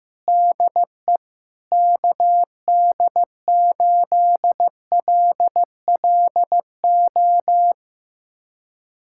Morse code, heard as a keyed tone, about 15 wpm, 700 Hz. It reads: DE KD8LLO
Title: USA Callsign DE Callsign - 6 Characters - 15wpm